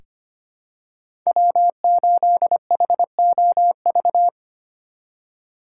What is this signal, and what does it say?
Morse code, heard as a keyed tone, about 25 wpm, 700 Hz. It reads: W8HOV
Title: USA Call Signs - 5 Characters - 4x Character Spacing - 25wpm